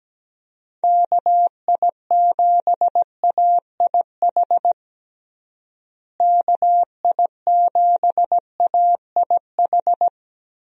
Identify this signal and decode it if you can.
Morse code, heard as a keyed tone, about 17 wpm, 700 Hz. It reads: KI7AIH KI7AIH
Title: USA Call Signs - 6 Characters - Repeated 3x - 17wpm